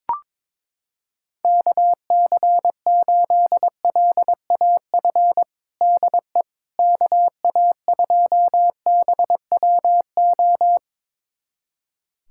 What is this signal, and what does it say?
Morse code, heard as a keyed tone, about 22 wpm, 700 Hz. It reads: KC8LAF DE KA2BWO
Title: USA Callsign DE Callsign - 6 Characters - 22wpm